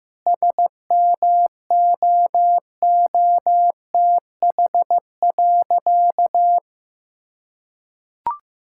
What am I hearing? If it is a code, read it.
Morse code, heard as a keyed tone, about 15 wpm, 700 Hz. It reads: SMOOTH.
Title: Sentences from Top 1600 Words 15wpm